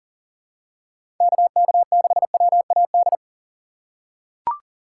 Morse code, heard as a keyed tone, about 40 wpm, 700 Hz. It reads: KK6WAD
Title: USA Call Signs - 6 Characters - 6x Character Spacing - 40wpm